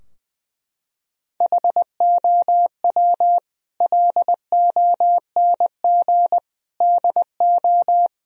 Morse code, heard as a keyed tone, about 20 wpm, 700 Hz. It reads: HOW LONG DO
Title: Sentences from Top 1000 Words - Review Part 2 - 20wpm